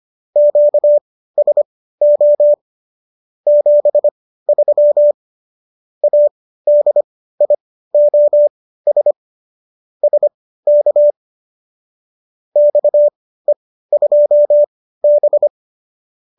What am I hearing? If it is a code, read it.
Morse code, heard as a keyed tone, about 25 wpm, spaced as for 15 wpm, 590 Hz. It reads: QSO 73 ADIOS SK XE2B
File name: qso204 15